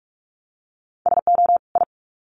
Morse code, heard as a keyed tone, about 44 wpm, 700 Hz. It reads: SO I
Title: Sets of 2 Words 45wpm